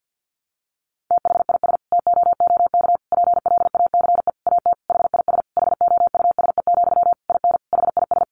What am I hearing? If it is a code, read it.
Morse code, heard as a keyed tone, about 50 wpm, 700 Hz. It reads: THIS TOOK PLACE AT HIS HOUSE, IN HIS
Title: Sentences from Top 300 Words 50wpm